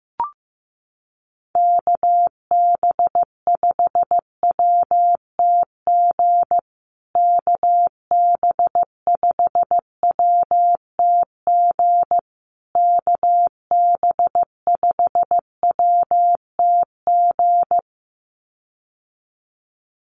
Morse code, heard as a keyed tone, about 15 wpm, 700 Hz. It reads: KB5WTG KB5WTG KB5WTG
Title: USA Call Signs Repeated 3x 15wpm